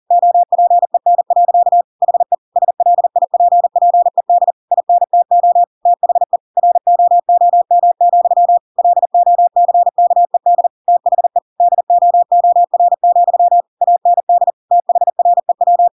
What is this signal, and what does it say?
Morse code, heard as a keyed tone, about 40 wpm, 700 Hz. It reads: OPEN. HE SLIPPED INTO THE ROOM, LOCKED THE DOOR, AND THREW
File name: Pinocchio / Pinocchio ch4 40 40 ik1hge